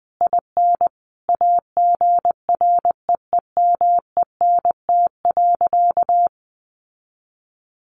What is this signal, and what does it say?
Morse code, heard as a keyed tone, about 20 wpm, 700 Hz. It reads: IN AGREEMENT.
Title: Sentences from Top 1500 Words 20wpm